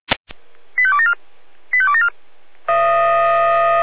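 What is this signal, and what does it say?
Five-tone selective call (ZVEI), heard as quick runs of five tones, tones of about 70 ms. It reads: 85274 85274
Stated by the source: ZVEI five-tone selective call received with an SDR.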